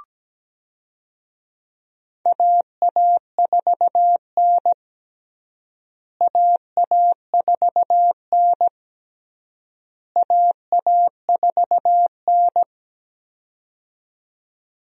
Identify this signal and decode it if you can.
Morse code, heard as a keyed tone, about 17 wpm, 700 Hz. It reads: AA4N AA4N AA4N
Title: USA Call Signs - 4 Characters - Repeated 3x - 17wpm